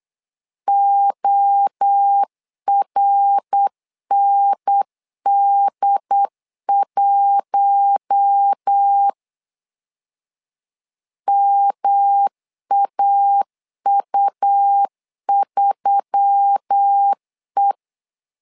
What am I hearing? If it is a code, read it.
Morse code, heard as a keyed tone, about 8 wpm, 800 Hz. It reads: ORND1 MAU3E
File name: LEZIONI / 4 Q1S3 8wpm 3037